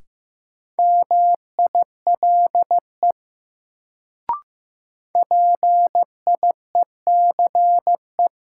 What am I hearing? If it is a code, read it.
Morse code, heard as a keyed tone, about 15 wpm, 700 Hz. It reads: MILE PIECE
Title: Sentences from Top 400 Words - New Word Drill - 15wpm